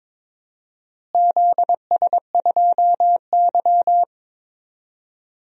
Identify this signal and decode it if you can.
Morse code, heard as a keyed tone, about 22 wpm, 700 Hz. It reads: ZS2Y
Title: DXCC-100 Call Signs - 4 Characters - Repeated 3x - 22wpm